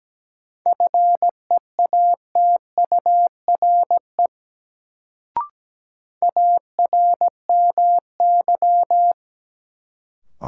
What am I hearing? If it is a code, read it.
Morse code, heard as a keyed tone, about 17 wpm, 700 Hz. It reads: FEATURE ARMY
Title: Sentences from Top 800 Words - New Word Drill - 17wpm